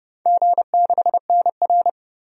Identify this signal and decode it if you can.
Morse code, heard as a keyed tone, about 30 wpm, 700 Hz. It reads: G6NR
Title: DXCC-100 Call Signs - 4 Characters - Repeated 3x - 30wpm